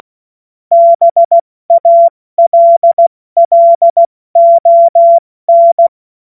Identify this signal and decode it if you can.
Morse code, heard as a keyed tone, about 16 wpm, 680 Hz. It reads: BALLON
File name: Morsekurs / Witze / Witz 03. Mai 2022 70 F80